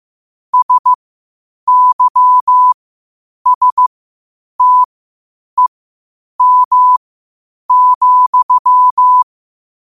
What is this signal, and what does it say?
Morse code, heard as a keyed tone, about 15 wpm, spaced as for 8 wpm, 1000 Hz. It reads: SYSTEM,
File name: qotdfiles / 8wpm